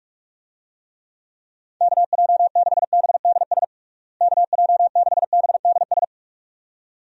Morse code, heard as a keyed tone, about 45 wpm, 700 Hz. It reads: KJ6BDS KJ6BDS
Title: USA Call Signs - 6 Characters - Repeated 2x - 45wpm